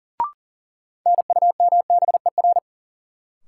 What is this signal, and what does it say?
Morse code, heard as a keyed tone, about 40 wpm, 700 Hz. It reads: NUMBER
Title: Sentences from Top 100 Words - New Word Drill - 40wpm